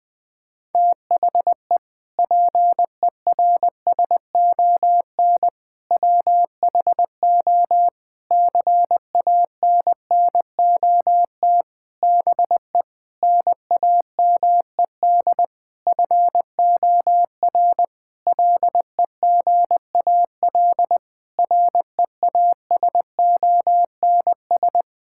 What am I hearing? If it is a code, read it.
Morse code, heard as a keyed tone, about 20 wpm, 700 Hz. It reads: THE PERSON WHO CANNOT BE NAMED FOR LEGAL REASONS
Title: Sentences with 8-Word Phrases - 20wpm